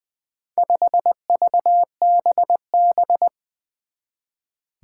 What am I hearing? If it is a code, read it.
Morse code, heard as a keyed tone, about 20 wpm, 700 Hz. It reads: 5VBB
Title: Four Challenging Letter-Number Combinations 20wpm